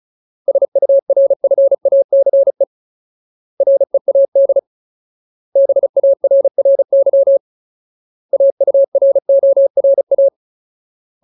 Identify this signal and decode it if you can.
Morse code, heard as a keyed tone, about 34 wpm, 550 Hz. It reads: SURFACE READ BARRY AURORA
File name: Ultimate Word List / 35WPM Ultimate